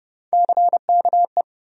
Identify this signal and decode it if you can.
Morse code, heard as a keyed tone, about 30 wpm, 700 Hz. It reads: CKE
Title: USA Call Sign Postfixes - 3 Letters 30wpm